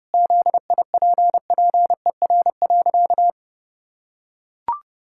Morse code, heard as a keyed tone, about 30 wpm, 700 Hz. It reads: ZIPPER.
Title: Pangram Challenge v2 - 30wpm